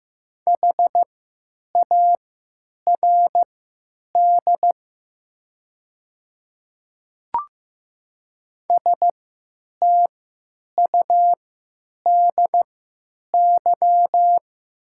Morse code, heard as a keyed tone, about 15 wpm, spaced as for 8 wpm, 700 Hz. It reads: HARD STUDY
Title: 3x Character Spacing - Top 300 Words - 15wpm